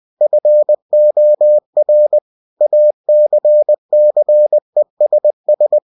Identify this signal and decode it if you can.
Morse code, heard as a keyed tone, about 20 wpm, 600 Hz. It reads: FOR ACCESS
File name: cw-books / winnie-the-pooh / winnie-20wpm-page-0116